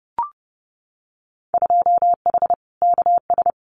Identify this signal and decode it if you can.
Morse code, heard as a keyed tone, about 30 wpm, 700 Hz. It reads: 2H KS
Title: Winter Field Day Exchanges Encore 30wpm